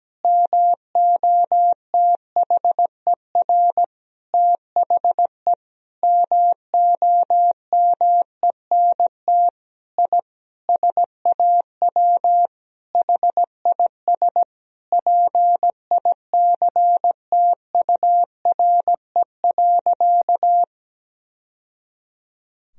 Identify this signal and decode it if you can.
Morse code, heard as a keyed tone, about 17 wpm, 700 Hz. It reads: MOTHER THE MOMENT I SAW HIS PICTURE.
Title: Sentences from Top 1000 Words - Review Part 1 - 17wpm